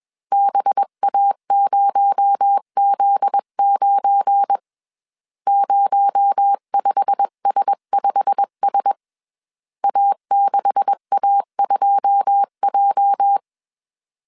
Morse code, heard as a keyed tone, about 21 wpm, 800 Hz. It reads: BA0Z9 05S5S A6A2J
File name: CW / DET / 1 9 1500C S B 20wpm 2313